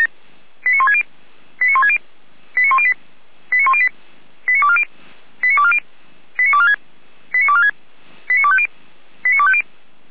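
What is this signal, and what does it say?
Five-tone selective call (ZVEI), heard as quick runs of five tones, tones of about 70 ms. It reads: 89160 89160 89108 89108 89240 89240 89246 89246 89250 89250
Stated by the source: ZVEI five-tone selective call received by SDR, address 89336.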